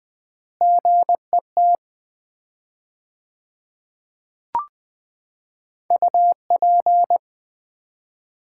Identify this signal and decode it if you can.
Morse code, heard as a keyed tone, about 21 wpm, 700 Hz. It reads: GET UP
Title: 4x Word-Spacing - Sets of 2 Words - 20wpm